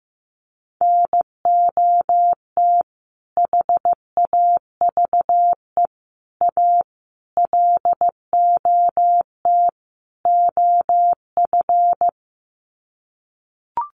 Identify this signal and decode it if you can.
Morse code, heard as a keyed tone, about 15 wpm, 700 Hz. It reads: NOT HAVE A LOT OF
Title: Sets of 5 Words 15wpm